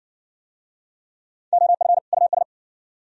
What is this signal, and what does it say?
Morse code, heard as a keyed tone, about 53 wpm, 700 Hz. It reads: OF US